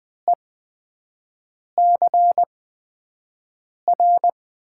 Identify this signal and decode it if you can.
Morse code, heard as a keyed tone, about 20 wpm, spaced as for 5 wpm, 700 Hz. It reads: ECR
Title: USA Call Signs - 6 Characters - 8x Character Spacing - 20wpm